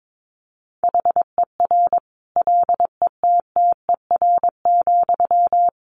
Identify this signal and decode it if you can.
Morse code, heard as a keyed tone, about 22 wpm, 700 Hz. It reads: HER LETTER,
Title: Sentences from Top 1100 Words 22wpm